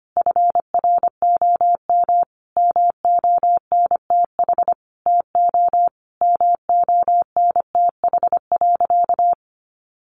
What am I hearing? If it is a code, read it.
Morse code, heard as a keyed tone, about 25 wpm, 700 Hz. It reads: FROM MONTH TO MONTH.
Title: Sentences from Top 1200 Words 25wpm